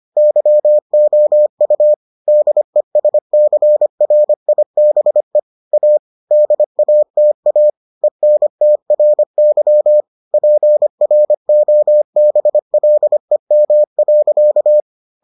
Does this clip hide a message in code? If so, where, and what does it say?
Yes: Morse code from 0.2 to 14.8 s, heard as a keyed tone, about 25 wpm, 600 Hz. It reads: YOU DESCRIBE A DATA ENTRY PROBLEM.